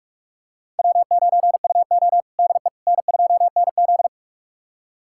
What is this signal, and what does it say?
Morse code, heard as a keyed tone, about 42 wpm, 700 Hz. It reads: W9UO DE N2NZ
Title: USA Callsign DE Callsign - 4 Characters - 45wpm